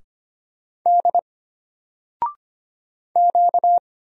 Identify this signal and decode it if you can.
Morse code, heard as a keyed tone, about 25 wpm, 700 Hz. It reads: D Q
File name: Single-Letter-Number-25wpm